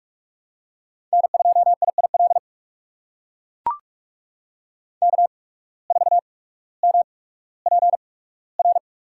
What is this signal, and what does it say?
Morse code, heard as a keyed tone, about 44 wpm, spaced as for 19 wpm, 700 Hz. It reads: N2IIL K4MPR
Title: USA Call Signs - 5 Characters - 8x Character Spacing - 45wpm